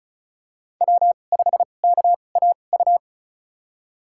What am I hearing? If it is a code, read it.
Morse code, heard as a keyed tone, about 36 wpm, spaced as for 25 wpm, 700 Hz. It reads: W5KAU
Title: USA Call Signs - 2x Character Spacing - 35wpm